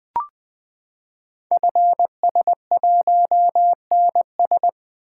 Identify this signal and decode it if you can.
Morse code, heard as a keyed tone, about 20 wpm, 700 Hz. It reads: FS1NS
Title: Code Groups - 5 Letter:Numbers - 20wpm